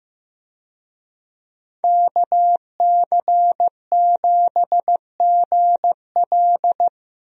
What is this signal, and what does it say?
Morse code, heard as a keyed tone, about 15 wpm, 700 Hz. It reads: KC7GL
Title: USA Call Signs - 5 Characters - Repeated 2x - 15wpm